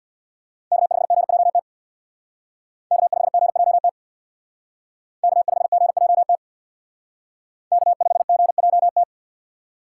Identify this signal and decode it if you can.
Morse code, heard as a keyed tone, about 62 wpm, 700 Hz. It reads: K5GJT K5GJT K5GJT K5GJT
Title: Speed Racing - USA Call Signs 50wpm